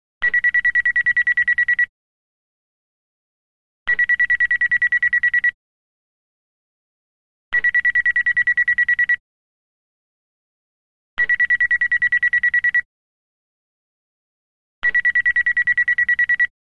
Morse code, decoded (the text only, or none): none